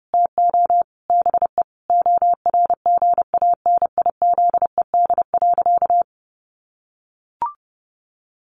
Morse code, heard as a keyed tone, about 30 wpm, 700 Hz. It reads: TO BE ORGANIZED.
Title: Sentences from Top 1300 Words 30wpm